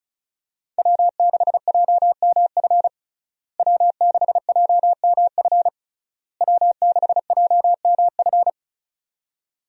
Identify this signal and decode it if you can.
Morse code, heard as a keyed tone, about 36 wpm, 700 Hz. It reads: W6JMF W6JMF W6JMF
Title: USA Call Signs - 5 Characters - Repeated 3x - 35wpm